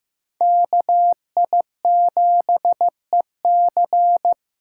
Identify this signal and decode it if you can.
Morse code, heard as a keyed tone, about 15 wpm, 700 Hz. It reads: KI7EC
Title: USA Call Signs - 2x Character Spacing - 15wpm